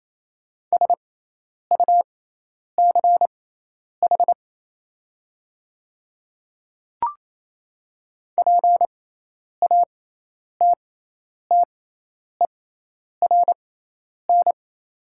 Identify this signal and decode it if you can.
Morse code, heard as a keyed tone, about 27 wpm, spaced as for 10 wpm, 700 Hz. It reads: SUCH PATTERN
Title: 6x Character Spacing - Top 500 Words - 28wpm